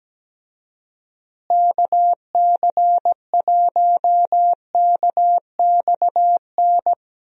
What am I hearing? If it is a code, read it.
Morse code, heard as a keyed tone, about 17 wpm, 700 Hz. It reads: KC1KXN
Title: USA Call Signs - 6 Characters - Repeated 2x - 17wpm